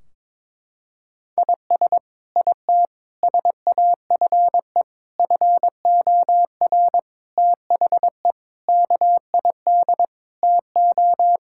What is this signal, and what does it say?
Morse code, heard as a keyed tone, about 22 wpm, 700 Hz. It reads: IS IT SAFE FOR THE KID TO
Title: Sentences from Top 1600 Words 22wpm